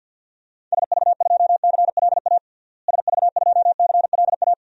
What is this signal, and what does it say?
Morse code, heard as a keyed tone, about 51 wpm, 700 Hz. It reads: SV2CLA SV2CLA
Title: DXCC-100 Call Signs - 6 Characters - Repeated 2x - 50wpm